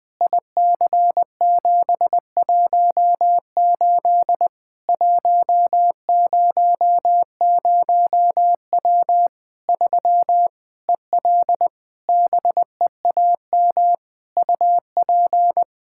Morse code, heard as a keyed tone, about 20 wpm, 700 Hz. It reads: IC718 100W 3 EL BEAM UP
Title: Practice QSO - Exchange 2 - Info - 20wpm